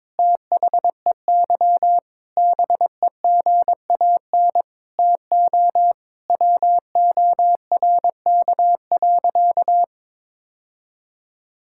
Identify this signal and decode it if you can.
Morse code, heard as a keyed tone, about 22 wpm, 700 Hz. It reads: THEY BEGAN TO WORK.
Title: Sentences from Top 1000 Words - Review Part 6 - 22wpm